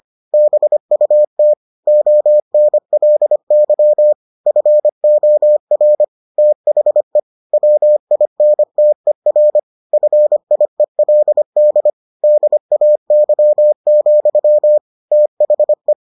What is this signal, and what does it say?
Morse code, heard as a keyed tone, about 25 wpm, 600 Hz. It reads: BUT ONLY FOR THE WINTER FIELD DAY, THE